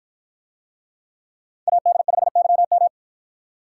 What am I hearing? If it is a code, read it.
Morse code, heard as a keyed tone, about 53 wpm, 700 Hz. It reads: AD5YM